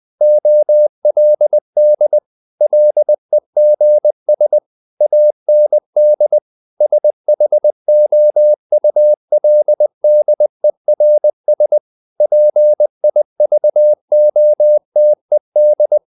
Morse code, heard as a keyed tone, about 20 wpm, 600 Hz. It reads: OLD LEGS AND SHOULDERS PIVOTED
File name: the old man and the sea part 2 20wpm